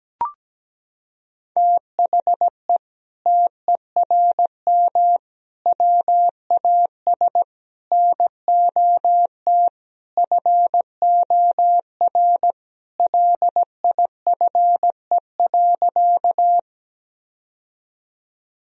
Morse code, heard as a keyed tone, about 17 wpm, 700 Hz. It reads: THE TERM WAS NOT FOR LIFE.
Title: Sentences from Top 1000 Words - Review Part 4 - 17wpm